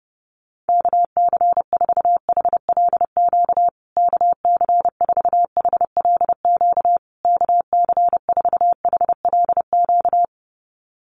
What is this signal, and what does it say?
Morse code, heard as a keyed tone, about 28 wpm, 700 Hz. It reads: KC4HLQ KC4HLQ KC4HLQ
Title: USA Call Signs Repeated 3x 30wpm